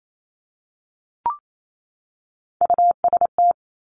Morse code, heard as a keyed tone, about 28 wpm, 700 Hz. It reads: UST
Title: Top Three Letter Combinations in Words 28wpm